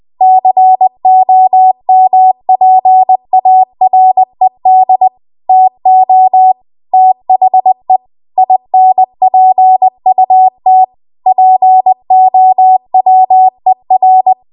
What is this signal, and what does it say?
Morse code, heard as a keyed tone, about 20 wpm, 750 Hz. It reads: COMPARED TO THE INPUT POWER
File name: Archive / 20 WPM / 20WPM